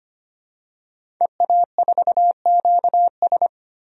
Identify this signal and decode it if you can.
Morse code, heard as a keyed tone, about 25 wpm, 700 Hz. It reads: EA4QS
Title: DXCC-100 Call Signs - 5 Characters - 25wpm